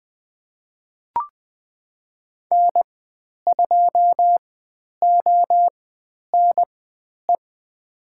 Morse code, heard as a keyed tone, about 20 wpm, spaced as for 10 wpm, 700 Hz. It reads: N2ONE
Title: 095 Lesson 16g - Letter c - Call Signs 20wpm